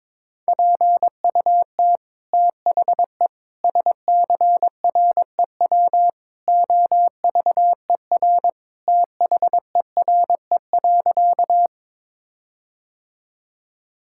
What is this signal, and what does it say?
Morse code, heard as a keyed tone, about 21 wpm, 700 Hz. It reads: PUT THE SCREW OVER THERE.
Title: Sentences from Top 1600 Words 22wpm